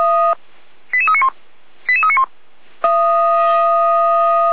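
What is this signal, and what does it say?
Five-tone selective call (ZVEI), heard as quick runs of five tones, tones of about 70 ms. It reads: 88391 88391
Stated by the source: ZVEI five-tone call received by SDR, audio.f85360.feuerwehr-orlamuende.org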